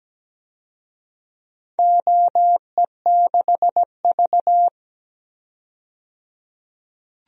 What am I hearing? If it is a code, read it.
Morse code, heard as a keyed tone, about 17 wpm, 700 Hz. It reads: OE6V
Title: DXCC-100 Call Signs - 4 Characters - 17wpm